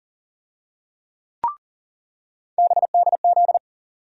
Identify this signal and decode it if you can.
Morse code, heard as a keyed tone, about 40 wpm, 700 Hz. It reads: BDZ
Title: USA Call Sign Postfixes - 3 Letters 40wpm